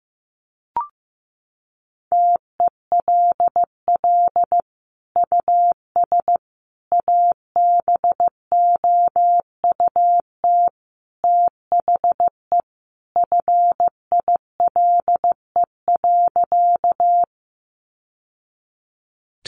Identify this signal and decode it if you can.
Morse code, heard as a keyed tone, about 15 wpm, 700 Hz. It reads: TELL US ABOUT THE FILE.
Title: Sentences from Top 1200 Words 15wpm